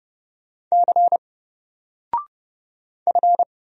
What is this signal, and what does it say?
Morse code, heard as a keyed tone, about 31 wpm, 700 Hz. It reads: C F
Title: Single Letter-Number 30wpm